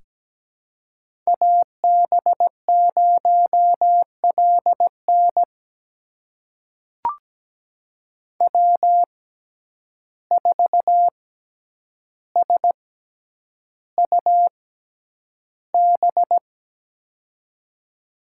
Morse code, heard as a keyed tone, about 17 wpm, spaced as for 9 wpm, 700 Hz. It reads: AB0LN W4SUB
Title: USA Call Signs - 5 Characters - 6x Character Spacing - 17wpm